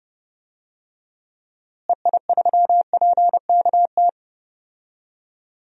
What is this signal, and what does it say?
Morse code, heard as a keyed tone, about 30 wpm, 700 Hz. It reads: EI3PKT
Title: DXCC-100 Call Signs - 6 Characters - Repeated 2x - 30wpm